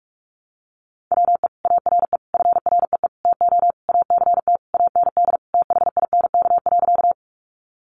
Morse code, heard as a keyed tone, about 44 wpm, 700 Hz. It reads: WE ARE FREE TO ACT AND THINK.
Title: Sentences from Top 500 Words 45wpm